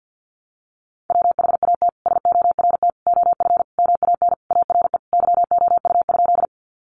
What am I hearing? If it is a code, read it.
Morse code, heard as a keyed tone, about 48 wpm, 700 Hz. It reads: WHAT SORT OF MAN ARE YOU?
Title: Sentences from Top 1300 Words 50wpm